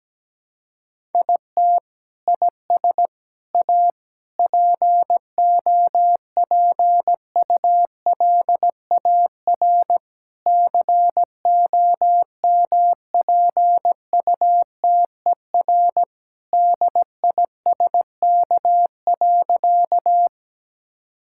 Morse code, heard as a keyed tone, about 17 wpm, 700 Hz. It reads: IT IS A POPULAR COMPUTER DISK.
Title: Sentences from Top 1000 Words - Review Part 6 - 17wpm